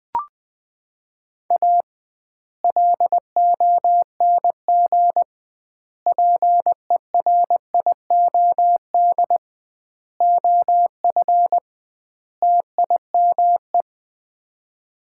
Morse code, heard as a keyed tone, about 20 wpm, 700 Hz. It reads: A LONG PERIOD OF TIME
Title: 2x Word-Spacing - Sets of 5 Words - 20wpm